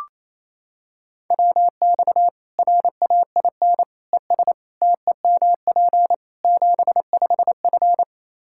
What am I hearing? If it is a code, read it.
Morse code, heard as a keyed tone, about 28 wpm, 700 Hz. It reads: WX RAIN ES TEMP 75F
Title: Practice QSO - Exchange 2 - Info - 28wpm